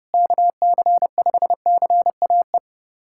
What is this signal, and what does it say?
Morse code, heard as a keyed tone, about 30 wpm, 700 Hz. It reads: KC5CAE
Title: USA Callsign DE Callsign - 6 Characters - 30wpm